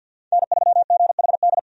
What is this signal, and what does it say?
Morse code, heard as a keyed tone, about 52 wpm, 700 Hz. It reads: N3GHD
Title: USA Call Signs - 5 Characters - 50wpm